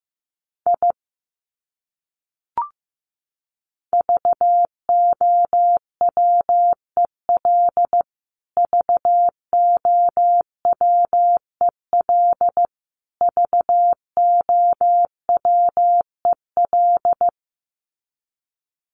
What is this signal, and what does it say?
Morse code, heard as a keyed tone, about 15 wpm, 700 Hz. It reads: I VOWEL VOWEL VOWEL
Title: Top 500 Words Repeated 15wpm